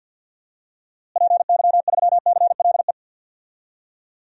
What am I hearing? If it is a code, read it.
Morse code, heard as a keyed tone, about 53 wpm, 700 Hz. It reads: PY3CLE
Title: DXCC-100 Call Signs - 6 Characters - Repeated 2x - 50wpm